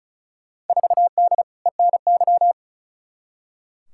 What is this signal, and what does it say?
Morse code, heard as a keyed tone, about 36 wpm, 700 Hz. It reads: 4D ENY
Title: Field Day Exchanges Encore - 35wpm